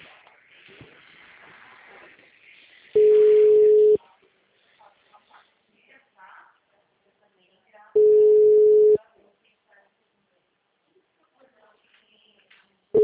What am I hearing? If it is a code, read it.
Ringback tone (Europe, 425 Hz), with speech.
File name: out- -2016- - - .56314